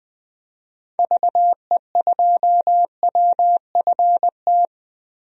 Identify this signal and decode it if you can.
Morse code, heard as a keyed tone, about 20 wpm, 700 Hz. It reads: VE2WFT
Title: DXCC-100 Call Signs - 6 Characters - Repeated 3x - 20wpm